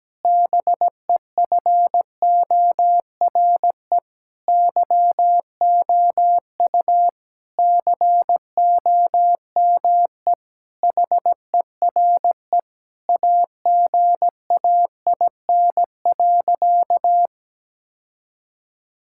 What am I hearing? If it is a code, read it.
Morse code, heard as a keyed tone, about 17 wpm, 700 Hz. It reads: BEFORE YOU COME HERE AGAIN.
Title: Sentences from Top 1600 Words 17wpm